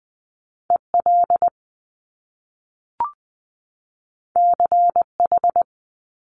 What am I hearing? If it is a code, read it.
Morse code, heard as a keyed tone, about 19 wpm, 700 Hz. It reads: E L C H
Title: Top Two Letter Combinations in Words 20wpm